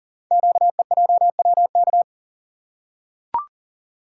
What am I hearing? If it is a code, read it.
Morse code, heard as a keyed tone, about 40 wpm, 700 Hz. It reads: QEJWK
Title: Code Groups - 5 Letters - 40wpm